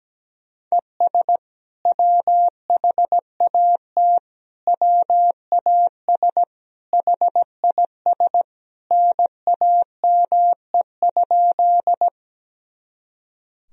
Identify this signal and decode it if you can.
Morse code, heard as a keyed tone, about 17 wpm, 700 Hz. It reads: ES WHAT WAS HIS NAME?
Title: Sentences from Top 200 Words and Abbreviations 17wpm